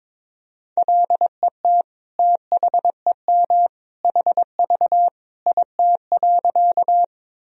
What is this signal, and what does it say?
Morse code, heard as a keyed tone, about 22 wpm, 700 Hz. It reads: LET THEM HV IT.
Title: Sentences from Top 300 Words and Abbreviations 22wpm